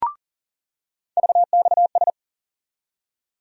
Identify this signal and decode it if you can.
Morse code, heard as a keyed tone, about 40 wpm, 700 Hz. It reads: VXS
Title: USA Call Sign Postfixes - 3 Letters 40wpm